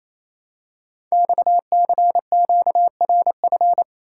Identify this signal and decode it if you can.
Morse code, heard as a keyed tone, about 27 wpm, 700 Hz. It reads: XCQRF
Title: Code Groups - 5 Letters - 28wpm